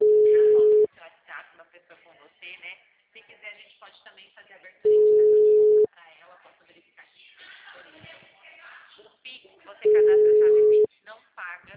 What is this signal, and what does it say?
Ringback tone (Europe, 425 Hz), with speech.
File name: out- -2009- - - .44447